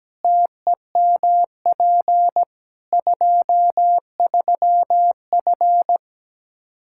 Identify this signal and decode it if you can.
Morse code, heard as a keyed tone, about 17 wpm, 700 Hz. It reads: TEMP 23F
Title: K1USN SST QSO Practice 17wpm